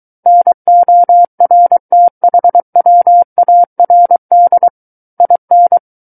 Morse code, heard as a keyed tone, about 23 wpm, 700 Hz. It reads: NORTHWARD IN
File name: Book One / 14-Chapter 14 - 23 WPM